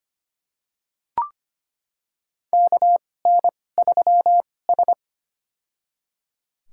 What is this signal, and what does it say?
Morse code, heard as a keyed tone, about 25 wpm, spaced as for 18 wpm, 700 Hz. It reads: KN3S
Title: USA Call Signs - 2x Character Spacing - 25wpm